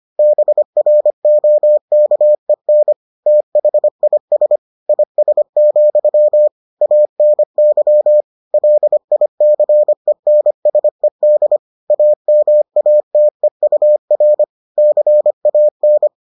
Morse code, heard as a keyed tone, about 25 wpm, 600 Hz. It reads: BROKEN THIS IS, ANY LICENSED AMATEUR CAN